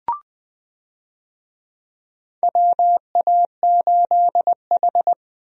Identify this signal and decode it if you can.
Morse code, heard as a keyed tone, about 20 wpm, 700 Hz. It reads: WA8H
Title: USA Call Signs - 4 Characters - Repeated 3x - 20wpm